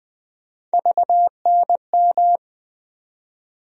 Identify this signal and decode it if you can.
Morse code, heard as a keyed tone, about 20 wpm, 700 Hz. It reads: VNM
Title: USA Call Sign Postfixes - 3 Letters 20wpm